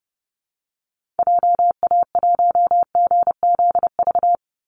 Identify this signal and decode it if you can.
Morse code, heard as a keyed tone, about 30 wpm, 700 Hz. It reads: JA1GZV
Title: Call Signs Repeated 2x - 30wpm